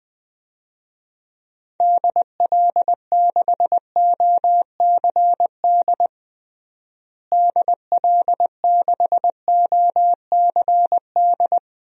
Morse code, heard as a keyed tone, about 20 wpm, 700 Hz. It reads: DL6OCD DL6OCD
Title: DXCC-100 Call Signs - 6 Characters - Repeated 3x - 20wpm